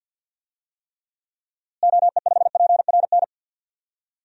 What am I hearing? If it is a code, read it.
Morse code, heard as a keyed tone, about 52 wpm, 700 Hz. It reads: OE5PRN
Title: DXCC-100 Call Signs - 6 Characters - Repeated 3x - 50wpm